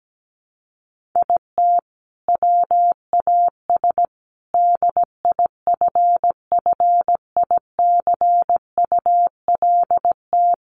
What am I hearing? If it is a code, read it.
Morse code, heard as a keyed tone, about 17 wpm, 700 Hz. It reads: IT WAS DIFFICULT
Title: Sentences from Top 1500 Words 17wpm